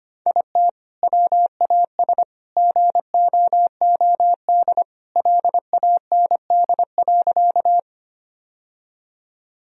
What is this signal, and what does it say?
Morse code, heard as a keyed tone, about 25 wpm, 700 Hz. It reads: IT WAS GOOD LAND.
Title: Sentences from Top 1000 Words - Review Part 1 - 25wpm